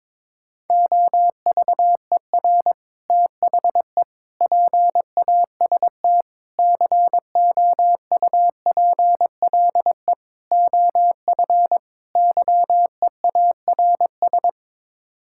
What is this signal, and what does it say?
Morse code, heard as a keyed tone, about 22 wpm, 700 Hz. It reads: OVER THE PAST COUPLE OF YEARS
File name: practice / 22wpm / Sets of 6 Words - 22wpm